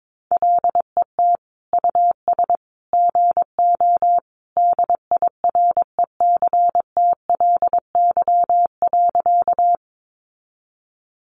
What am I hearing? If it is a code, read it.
Morse code, heard as a keyed tone, about 22 wpm, 700 Hz. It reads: LET US GO DIRECTLY.
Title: Sentences from Top 800 Words 22wpm